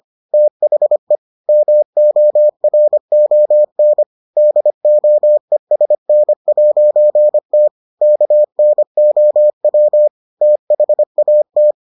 Morse code, heard as a keyed tone, about 25 wpm, 600 Hz. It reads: THE MORON DOESN'T KNOW THAT